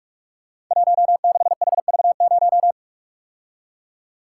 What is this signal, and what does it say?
Morse code, heard as a keyed tone, about 44 wpm, 700 Hz. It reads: 16HV0
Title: Code Groups - 5 Letter:Numbers - 45wpm